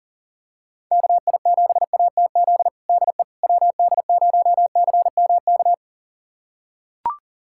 Morse code, heard as a keyed tone, about 40 wpm, 700 Hz. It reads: KI7ATZ DE WD0CMK
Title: USA Callsign DE Callsign - 6 Characters - 40wpm